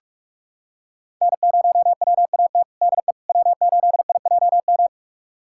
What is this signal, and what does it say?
Morse code, heard as a keyed tone, about 43 wpm, 700 Hz. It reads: N0WAT DE W8IJM
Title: USA Callsign DE Callsign - 5 Characters - 45wpm